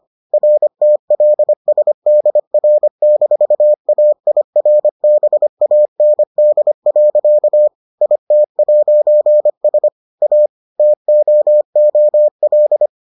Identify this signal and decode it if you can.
Morse code, heard as a keyed tone, about 25 wpm, 600 Hz. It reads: RTLSDR-AIRBAND. IT'S A TOOL